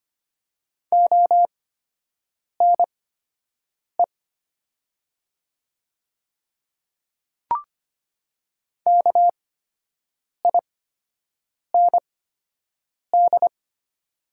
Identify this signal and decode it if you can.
Morse code, heard as a keyed tone, about 25 wpm, spaced as for 7 wpm, 700 Hz. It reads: ONE KIND
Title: 8x Character Spacing - Top 200 Words - 25wpm